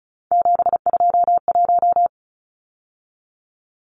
Morse code, heard as a keyed tone, about 34 wpm, 700 Hz. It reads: 721
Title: Numbers - 3-Digit 35wpm